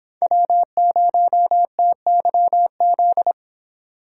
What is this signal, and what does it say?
Morse code, heard as a keyed tone, about 26 wpm, 700 Hz. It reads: W0TYZ